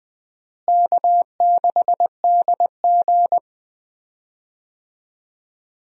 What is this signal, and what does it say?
Morse code, heard as a keyed tone, about 20 wpm, 700 Hz. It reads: K6DG
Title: USA Call Signs - 4 Characters - Repeated 3x - 20wpm